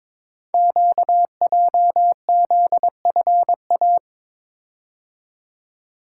Morse code, heard as a keyed tone, about 22 wpm, 700 Hz. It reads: QJZFA
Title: Code Groups - 5 Letters - 22wpm